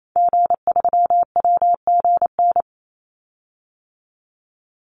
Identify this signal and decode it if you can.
Morse code, heard as a keyed tone, about 28 wpm, 700 Hz. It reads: G3WGN
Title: Call Signs - 28wpm